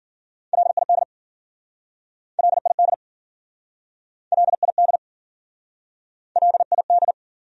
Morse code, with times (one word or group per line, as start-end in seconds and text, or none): 0.5-1.0 LID
2.4-2.9 LID
4.3-5.0 LID
6.4-7.1 LID